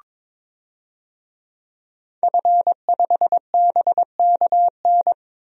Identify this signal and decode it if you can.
Morse code, heard as a keyed tone, about 22 wpm, 700 Hz. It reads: F5BKN
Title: DXCC-100 Call Signs - 5 Characters - 22wpm